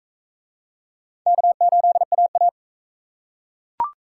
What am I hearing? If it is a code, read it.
Morse code, heard as a keyed tone, about 38 wpm, 700 Hz. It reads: K8AA